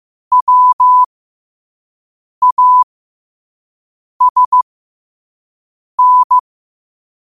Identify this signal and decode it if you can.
Morse code, heard as a keyed tone, about 15 wpm, spaced as for 5 wpm, 1000 Hz. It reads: WASN